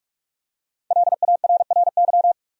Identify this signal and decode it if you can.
Morse code, heard as a keyed tone, about 43 wpm, 700 Hz. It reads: LARRY
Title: USA Common Names - 45wpm